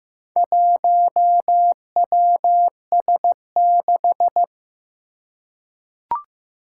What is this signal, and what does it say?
Morse code, heard as a keyed tone, about 15 wpm, 700 Hz. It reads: 1WS6
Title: Four Challenging Letter-Number Combinations 15wpm